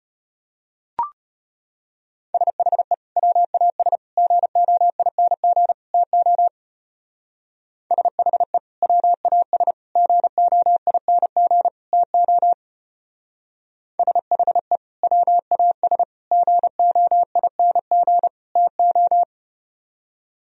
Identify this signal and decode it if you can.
Morse code, heard as a keyed tone, about 32 wpm, 700 Hz. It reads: SHE WAS GOING TO SHE WAS GOING TO SHE WAS GOING TO